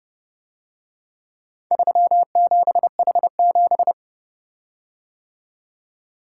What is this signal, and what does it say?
Morse code, heard as a keyed tone, about 30 wpm, 700 Hz. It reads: 37H7